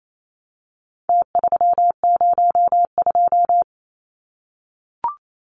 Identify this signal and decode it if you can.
Morse code, heard as a keyed tone, about 28 wpm, 700 Hz. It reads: T302
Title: Numbers - Contest Style - 4-Digit 28wpm